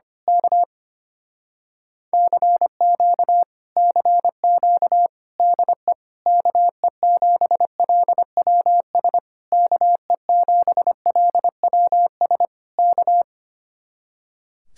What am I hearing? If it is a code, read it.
Morse code, heard as a keyed tone, about 24 wpm, 700 Hz. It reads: K CQ CQ DE KE7LWS KE7LWS K
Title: Speed-Racing Common QSO Phrases Encore - 25wpm